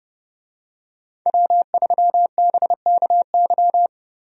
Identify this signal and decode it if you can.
Morse code, heard as a keyed tone, about 32 wpm, 700 Hz. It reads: W3BKY
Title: USA Call Signs - 5 Characters - Repeated 2x - 30wpm